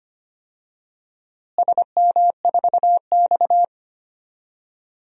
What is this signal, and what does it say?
Morse code, heard as a keyed tone, about 25 wpm, 700 Hz. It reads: SM4X
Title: DXCC-100 Call Signs - 4 Characters - Repeated 3x - 25wpm